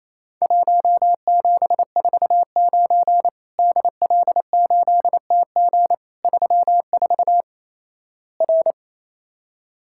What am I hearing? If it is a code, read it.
Morse code, heard as a keyed tone, about 28 wpm, 700 Hz. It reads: 1749 DL8TG 34 R
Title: Worked All Europe DX Contest - QTC Exchange - 28wpm